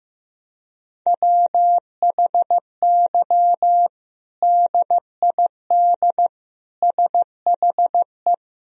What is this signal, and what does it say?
Morse code, heard as a keyed tone, about 16 wpm, 700 Hz. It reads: WHY DID SHE
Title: Sentences from Top 1000 Words - Review Part 3 - 15wpm